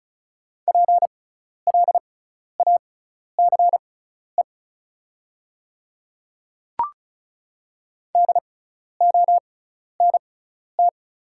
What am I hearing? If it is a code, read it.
Morse code, heard as a keyed tone, about 36 wpm, spaced as for 12 wpm, 690 Hz. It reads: PLACE DONT
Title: 6x Character Spacing - Top 300 Words - 35wpm